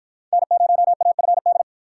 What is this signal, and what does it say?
Morse code, heard as a keyed tone, about 53 wpm, 700 Hz. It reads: N0AFD